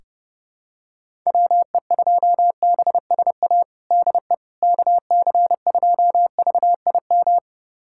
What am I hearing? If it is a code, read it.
Morse code, heard as a keyed tone, about 30 wpm, 700 Hz. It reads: WE2BSA DE KC2VIM
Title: USA Callsign DE Callsign - 6 Characters - 30wpm